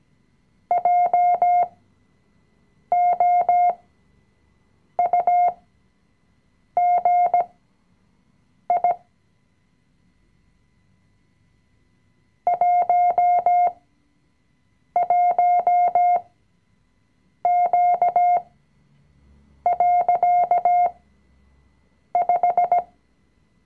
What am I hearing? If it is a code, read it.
Morse code, heard as a keyed tone, about 17 wpm, spaced as for 6 wpm, 710 Hz. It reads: JOUGI 11Q.5
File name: CW-Online-Kurs / Anfänger / Lektion 020